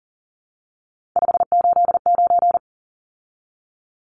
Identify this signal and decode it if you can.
Morse code, heard as a keyed tone, about 40 wpm, 700 Hz. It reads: 589
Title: Numbers - 3-Digit 40wpm